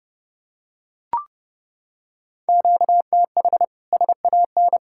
Morse code, heard as a keyed tone, about 30 wpm, 700 Hz. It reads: QTH SAN
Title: Practice QSO - Exchange 1 - Beginning - 30wpm